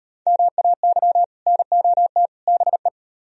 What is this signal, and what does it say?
Morse code, heard as a keyed tone, about 38 wpm, 700 Hz. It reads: MAY NOT BE